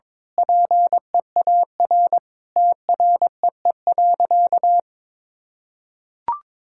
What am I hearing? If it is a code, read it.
Morse code, heard as a keyed tone, about 22 wpm, 700 Hz. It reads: PEAR TREE.
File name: Sentences with 9-Word Phrases - 22wpm